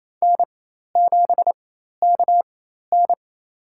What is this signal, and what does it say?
Morse code, heard as a keyed tone, about 28 wpm, spaced as for 13 wpm, 700 Hz. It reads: N7KN
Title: USA Call Signs - 4 Characters - 4x Character Spacing - 28wpm